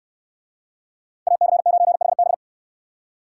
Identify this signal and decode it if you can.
Morse code, heard as a keyed tone, about 62 wpm, 700 Hz. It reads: AC9HB